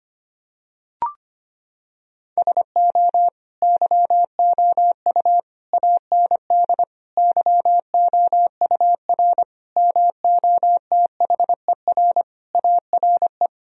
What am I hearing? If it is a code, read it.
Morse code, heard as a keyed tone, about 25 wpm, 700 Hz. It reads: SO YOU AND YOUR MOTHER ARE
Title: Sentences from Top 1000 Words - Review Part 4 - 25wpm